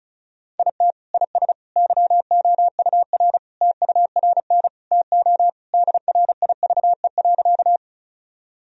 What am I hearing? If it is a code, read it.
Morse code, heard as a keyed tone, about 36 wpm, 700 Hz. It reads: IT IS YOUR TURN TO DRIVE.
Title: Sentences from Top 1000 Words - Review Part 2 - 35wpm